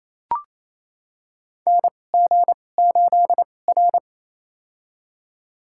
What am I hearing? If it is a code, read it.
Morse code, heard as a keyed tone, about 28 wpm, spaced as for 20 wpm, 700 Hz. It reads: NG8R
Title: USA Call Signs - 4 Characters - 2x Character Spacing - 28wpm